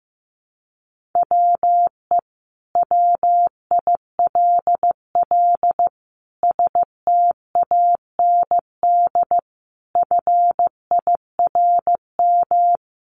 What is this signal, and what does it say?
Morse code, heard as a keyed tone, about 15 wpm, 700 Hz. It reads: WE WILL STAND FIRM
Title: Sentences from Top 1400 Words 15wpm